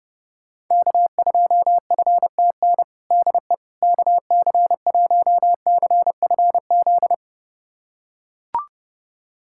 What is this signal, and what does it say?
Morse code, heard as a keyed tone, about 30 wpm, 700 Hz. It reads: K2FTN DE KC1CFZ
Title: USA Callsign DE Callsign 30wpm